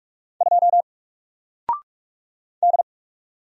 Morse code, heard as a keyed tone, about 46 wpm, 700 Hz. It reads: 2 D
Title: Single Letter-Number Alt 45wpm